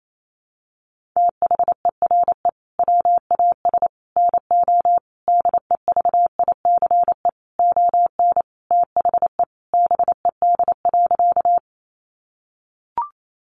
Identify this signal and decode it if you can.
Morse code, heard as a keyed tone, about 28 wpm, 700 Hz. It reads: THERE WAS NO DEVICE ON THE BED.
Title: Sentences from Top 1500 Words 28wpm